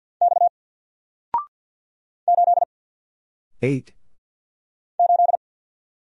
Morse code, heard as a keyed tone, about 50 wpm, 700 Hz. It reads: X 8 8
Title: Single Character (Letter:Number:Punctuation:Prosign) 50wpm